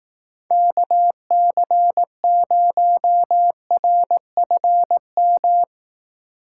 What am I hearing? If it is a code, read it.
Morse code, heard as a keyed tone, about 18 wpm, 700 Hz. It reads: KC0RFM
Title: Speed Racing - USA 2x3 Call Signs 15wpm